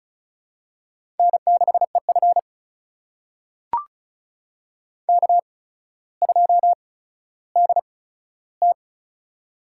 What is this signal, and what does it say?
Morse code, heard as a keyed tone, about 36 wpm, spaced as for 15 wpm, 700 Hz. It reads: N6EF K2DT
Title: USA Call Signs - 8x Character Spacing - 35wpm